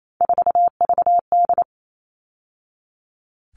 Morse code, heard as a keyed tone, about 28 wpm, 700 Hz. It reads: 4VD
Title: All Cut Numbers - 3-Digit 28wpm